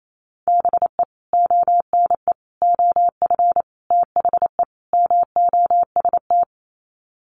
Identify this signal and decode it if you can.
Morse code, heard as a keyed tone, about 28 wpm, 700 Hz. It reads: BE ONE OF THE MOST
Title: Sets of 5 Words Encore 28wpm